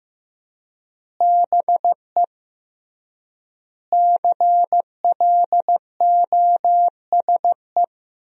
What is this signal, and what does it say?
Morse code, heard as a keyed tone, about 16 wpm, 700 Hz. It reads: BE CLOSE
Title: 3x Word-Spacing - Sentences from Top 300 Words - 15wpm